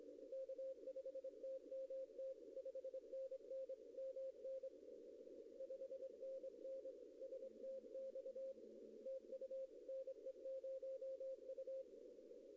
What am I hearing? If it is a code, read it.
Morse code, heard as a keyed tone, about 24 wpm, 530 Hz. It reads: K5TMT 5NN MN 5NN STX TU NE0U
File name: contests / arrl-160-k5tr-op-w5tn -0100